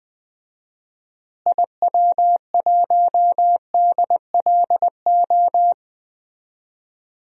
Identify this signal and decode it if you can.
Morse code, heard as a keyed tone, about 21 wpm, 700 Hz. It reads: IW1DLO
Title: DXCC-100 Call Signs - 6 Characters - Repeated 2x - 20wpm